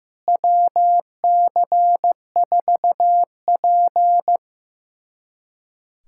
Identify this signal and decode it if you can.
Morse code, heard as a keyed tone, about 16 wpm, 700 Hz. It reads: WC4P
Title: USA Callsign DE Callsign - 4 Characters - 15wpm